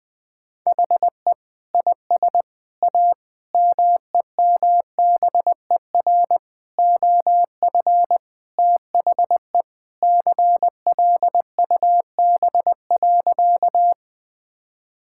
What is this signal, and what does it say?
Morse code, heard as a keyed tone, about 20 wpm, 700 Hz. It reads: HE IS A MEMBER OF THE CLUB.
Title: Sentences with 3-Word Phrases - Spoken Beginning - 20wpm